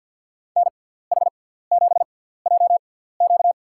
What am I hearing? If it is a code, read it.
Morse code, heard as a keyed tone, about 52 wpm, spaced as for 17 wpm, 700 Hz. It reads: NH7JQ
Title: USA Call Signs - 5 Characters - 6x Character Spacing - 50wpm